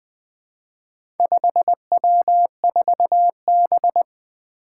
Morse code, heard as a keyed tone, about 20 wpm, 700 Hz. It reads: 5W4B
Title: Four Letter-Number Combinations - 20wpm